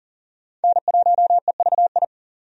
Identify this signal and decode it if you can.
Morse code, heard as a keyed tone, about 40 wpm, 700 Hz. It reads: N1EVI
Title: USA Call Signs - 5 Characters - 40wpm